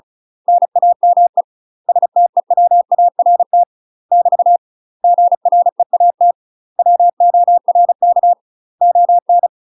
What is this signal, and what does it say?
Morse code, heard as a keyed tone, about 36 wpm, 700 Hz. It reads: NAME STEWART <BT> GREAT WORK ON
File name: UK MASTER QSOs TRAINING / 35WPM / UK MASTER QSOs 0010